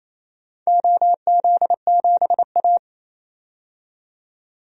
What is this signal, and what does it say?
Morse code, heard as a keyed tone, about 28 wpm, 700 Hz. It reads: OZ7A
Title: DXCC-100 Call Signs - 4 Characters - 28wpm